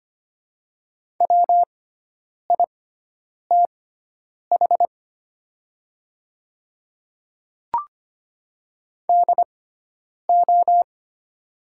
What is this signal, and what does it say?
Morse code, heard as a keyed tone, about 25 wpm, spaced as for 9 wpm, 700 Hz. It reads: WITH DO
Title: 6x Character Spacing - Top 100 Words - 25wpm